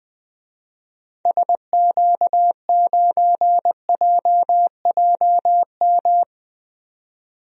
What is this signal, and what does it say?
Morse code, heard as a keyed tone, about 20 wpm, 700 Hz. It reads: SQ9JJM
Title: DXCC-100 Call Signs - 6 Characters - 20wpm